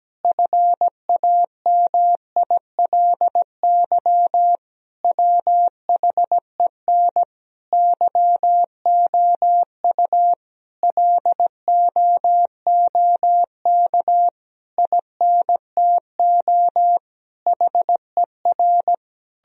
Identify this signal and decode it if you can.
Morse code, heard as a keyed tone, about 17 wpm, 700 Hz. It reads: FAMILY WHEN YOU LOOK INTO HER
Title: Sentences from Top 1000 Words - Review Part 4 - 17wpm